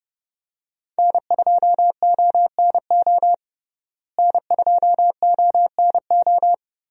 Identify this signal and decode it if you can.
Morse code, heard as a keyed tone, about 31 wpm, 700 Hz. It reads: N2ONO N2ONO
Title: USA Call Signs - 5 Characters - Repeated 3x - 30wpm